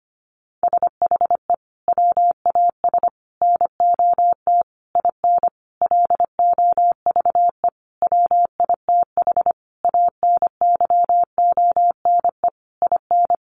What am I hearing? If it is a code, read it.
Morse code, heard as a keyed tone, about 25 wpm, 700 Hz. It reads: SHE WAS NOT IN LOVE WITH ANYONE IN
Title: Sentences from Top 900 Words 25wpm